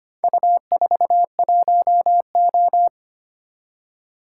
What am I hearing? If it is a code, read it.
Morse code, heard as a keyed tone, about 25 wpm, 700 Hz. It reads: U41O
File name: practice / 25wpm / Four Letter-Number Combinations - 25wpm